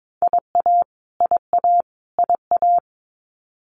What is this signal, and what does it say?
Morse code, heard as a keyed tone, about 22 wpm, 700 Hz. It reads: IA IA IA
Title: US States Repeated 3x 22wpm